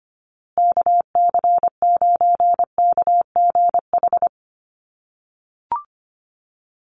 Morse code, heard as a keyed tone, about 25 wpm, 700 Hz. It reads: KC9KGH
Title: USA Call Signs Repeated 3x 25wpm